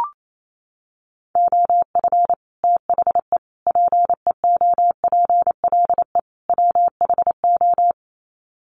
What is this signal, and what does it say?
Morse code, heard as a keyed tone, about 28 wpm, 700 Hz. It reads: OF THE PEOPLE WHO
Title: Sets of 4 Words 28wpm